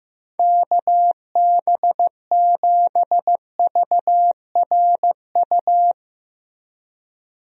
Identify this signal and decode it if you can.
Morse code, heard as a keyed tone, about 15 wpm, 700 Hz. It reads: KB7VRU
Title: USA Call Signs - 6 Characters - Repeated 2x - 15wpm